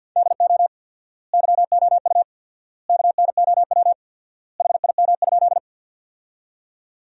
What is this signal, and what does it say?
Morse code, heard as a keyed tone, about 52 wpm, 700 Hz. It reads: DO YOU KNOW HIM?
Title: 4x Word-Spacing - Sentences from Top 100 Words - 50wpm